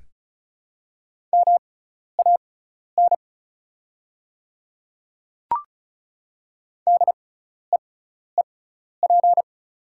Morse code, heard as a keyed tone, about 34 wpm, spaced as for 12 wpm, 690 Hz. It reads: MAN DEEP
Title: 6x Character Spacing - Top 500 Words - 35wpm